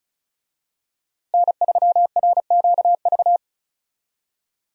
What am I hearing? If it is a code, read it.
Morse code, heard as a keyed tone, about 34 wpm, 700 Hz. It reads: N3RQV
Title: USA Call Signs - 5 Characters - Repeated 3x - 35wpm